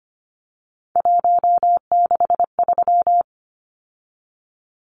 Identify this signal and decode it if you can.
Morse code, heard as a keyed tone, about 25 wpm, 700 Hz. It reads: 163
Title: Numbers - 3-Digit 25wpm